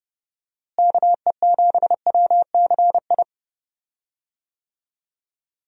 Morse code, heard as a keyed tone, about 30 wpm, 700 Hz. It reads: KE7WCI
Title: USA Call Signs - 6 Characters - Repeated 3x - 30wpm